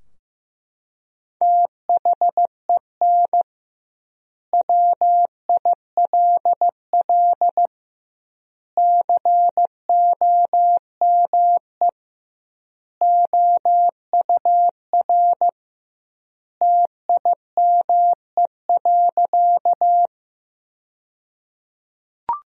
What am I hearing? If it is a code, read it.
Morse code, heard as a keyed tone, about 15 wpm, 700 Hz. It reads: THEN WILL COME OUR TIME.
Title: 2x Word-Spacing - Sentences from Top 200 Words - 15wpm